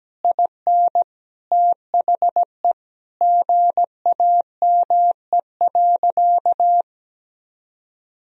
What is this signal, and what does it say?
Morse code, heard as a keyed tone, about 17 wpm, 700 Hz. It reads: IN THE GAME.
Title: Sentences from Top 1000 Words - Review Part 4 - 17wpm